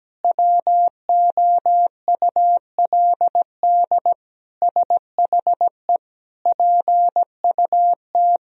Morse code, heard as a keyed tone, about 17 wpm, 700 Hz. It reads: WOULD SHE PUT
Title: Sentences from Top 1000 Words - Review Part 1 - 17wpm